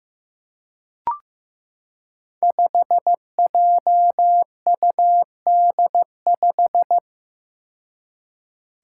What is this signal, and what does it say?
Morse code, heard as a keyed tone, about 15 wpm, 700 Hz. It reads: 5JUD5
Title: Five Challenging Letter-Number Combinations 15wpm